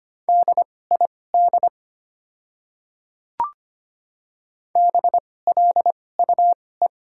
Morse code, heard as a keyed tone, about 25 wpm, spaced as for 18 wpm, 700 Hz. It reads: DID BLUE
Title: 2x Character Spacing - Top 500 Words - 25wpm